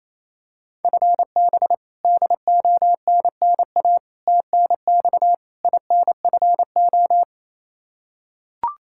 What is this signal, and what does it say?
Morse code, heard as a keyed tone, about 28 wpm, 700 Hz. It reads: FB DONNA TNX INFO
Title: Practice QSO - Exchange 3 - More Info - 28wpm